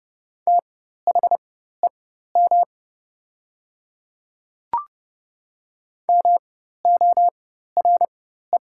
Morse code, heard as a keyed tone, about 30 wpm, spaced as for 14 wpm, 700 Hz. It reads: THEM MORE
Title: 4x Character Spacing - Top 100 Words - 30wpm